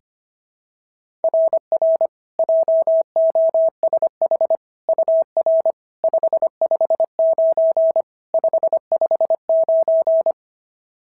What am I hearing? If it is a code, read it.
Morse code, heard as a keyed tone, about 23 wpm, 650 Hz. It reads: RR JOSH UR 559 559
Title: SOTA QSO Practice 25wpm